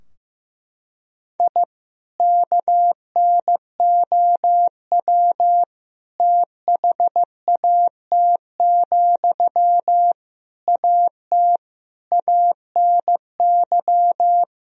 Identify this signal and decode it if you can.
Morse code, heard as a keyed tone, about 15 wpm, 700 Hz. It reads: I KNOW THAT, AT ANY
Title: Sentences from Top 1000 Words - Review Part 4 - 15wpm